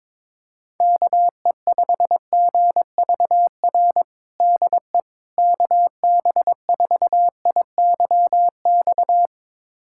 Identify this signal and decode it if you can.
Morse code, heard as a keyed tone, about 21 wpm, 700 Hz. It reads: KE5GVR DE KB4IYX
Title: USA Callsign DE Callsign 22wpm